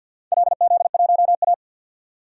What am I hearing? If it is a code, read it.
Morse code, heard as a keyed tone, about 52 wpm, 700 Hz. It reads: LZ1A
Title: DXCC-100 Call Signs - 4 Characters - 50wpm